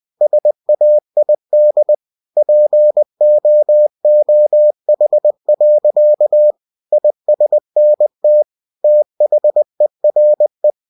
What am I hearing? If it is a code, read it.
Morse code, heard as a keyed tone, about 20 wpm, 600 Hz. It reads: SAID POOH. ISNT THERE
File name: cw-books / winnie-the-pooh / winnie-20wpm-page-0013